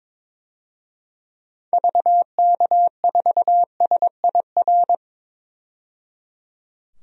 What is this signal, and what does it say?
Morse code, heard as a keyed tone, about 22 wpm, 700 Hz. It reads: VK4SIR
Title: DXCC-100 Call Signs - 6 Characters - 22wpm